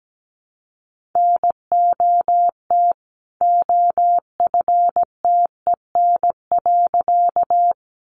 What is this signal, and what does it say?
Morse code, heard as a keyed tone, about 17 wpm, 700 Hz. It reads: NOT OFTEN.
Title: A Sound Of Thunder 17wpm